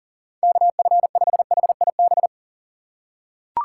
Morse code, heard as a keyed tone, about 39 wpm, 700 Hz. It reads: KF5HIB
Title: USA Call Signs - 6 Characters - 3x Character Spacing - 40wpm